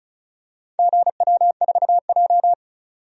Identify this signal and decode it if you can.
Morse code, heard as a keyed tone, about 36 wpm, 700 Hz. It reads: GW4J
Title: DXCC-100 Call Signs - 4 Characters - 35wpm